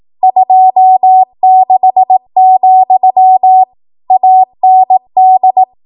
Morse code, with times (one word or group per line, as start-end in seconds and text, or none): 0.2-3.6 26,
4.1-5.6 AND